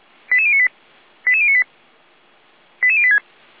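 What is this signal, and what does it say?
Five-tone selective call (ZVEI), heard as quick runs of five tones, tones of about 70 ms. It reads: 88098 88098 88086
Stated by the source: Five-tone ZVEI selective call 88571, received by SDR.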